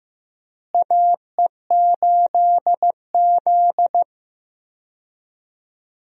Morse code, heard as a keyed tone, about 15 wpm, 700 Hz. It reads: AE8Z
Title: USA Call Signs - 4 Characters - Repeated 2x - 15wpm